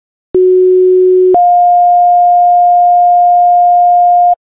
Two-tone page, heard as a one-second tone, then a three-second tone, about 370 then 710 Hz. It is heard from 0.3 to 4.4 s.